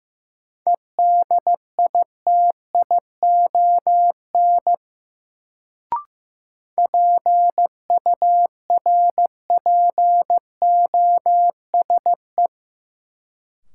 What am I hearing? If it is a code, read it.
Morse code, heard as a keyed tone, about 15 wpm, 700 Hz. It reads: EDITION PURPOSE
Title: Sets of 5 Words - New Word Drill - 15wpm